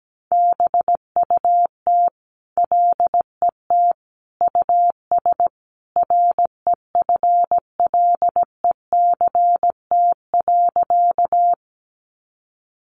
Morse code, heard as a keyed tone, about 17 wpm, 700 Hz. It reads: BUT LET US REFLECT.
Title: Sentences from Top 1500 Words 17wpm